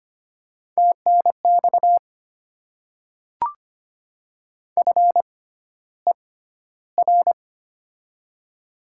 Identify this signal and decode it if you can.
Morse code, heard as a keyed tone, about 25 wpm, spaced as for 13 wpm, 700 Hz. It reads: TNX FER
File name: Top 20 QSO Elements - 6x Character Spacing - 25wpm